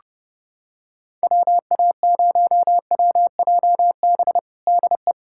Morse code, heard as a keyed tone, about 30 wpm, 700 Hz. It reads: WA0WJB DE
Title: USA Callsign DE Callsign - 6 Characters - 30wpm